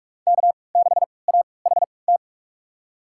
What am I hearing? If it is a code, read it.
Morse code, heard as a keyed tone, about 44 wpm, spaced as for 25 wpm, 700 Hz. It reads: K6AHT
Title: USA Call Signs - 3x Character Spacing - 45wpm